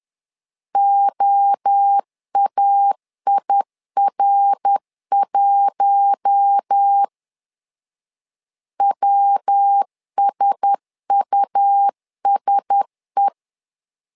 Morse code, heard as a keyed tone, about 10 wpm, 800 Hz. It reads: OAIR1 WSUSE
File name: LEZIONI / 4 Q1S 10wpm 2133